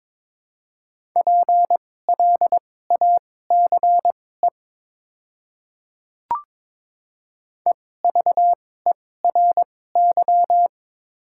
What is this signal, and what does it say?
Morse code, heard as a keyed tone, about 23 wpm, spaced as for 16 wpm, 700 Hz. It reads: PLACE EVERY
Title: 2x Character Spacing - Top 300 Words - 22wpm